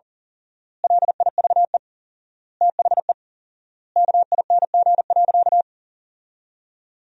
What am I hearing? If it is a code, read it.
Morse code, heard as a keyed tone, about 40 wpm, 700 Hz. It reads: LIVE THE KING.
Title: 4x Word-Spacing - Sentences from Top 400 Words - 40wpm